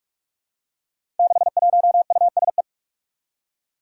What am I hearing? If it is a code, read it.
Morse code, heard as a keyed tone, about 42 wpm, 700 Hz. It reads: 61USE
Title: Five Challenging Letter-Number Combinations 45wpm